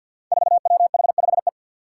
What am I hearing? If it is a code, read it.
Morse code, heard as a keyed tone, about 46 wpm, 700 Hz. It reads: 4WH5E
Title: Five Challenging Letter-Number Combinations 50wpm